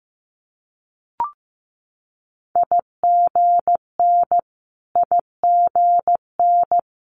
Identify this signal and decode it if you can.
Morse code, heard as a keyed tone, about 15 wpm, 700 Hz. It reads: IGN IGN
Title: Multi-letter Phonemes Repeated 3x 15wpm